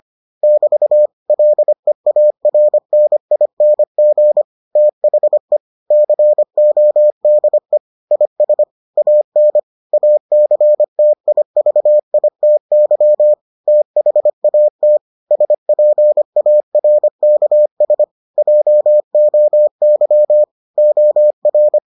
Morse code, heard as a keyed tone, about 25 wpm, 600 Hz. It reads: <BT> LEARNING THE CODE IS AN ACTIVITY THAT SPARKS JOY OR